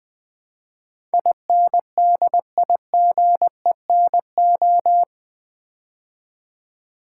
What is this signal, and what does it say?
Morse code, heard as a keyed tone, about 20 wpm, 700 Hz. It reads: INDIGENO
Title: Build A Word - 4 Letters At A Time - 10-Letter Words - 20wpm